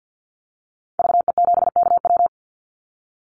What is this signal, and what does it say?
Morse code, heard as a keyed tone, about 50 wpm, 700 Hz. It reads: VE7KW
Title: Call Signs - 50wpm